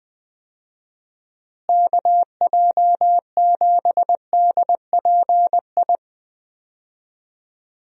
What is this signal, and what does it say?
Morse code, heard as a keyed tone, about 20 wpm, 700 Hz. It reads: KJ7DPI
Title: USA Call Signs - 6 Characters - Repeated 3x - 20wpm